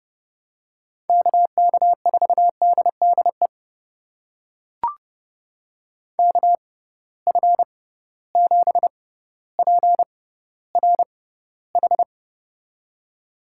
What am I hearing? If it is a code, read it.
Morse code, heard as a keyed tone, about 30 wpm, spaced as for 15 wpm, 700 Hz. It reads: KK4DDE KF7PRH
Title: USA Call Signs - 6x Character Spacing - 30wpm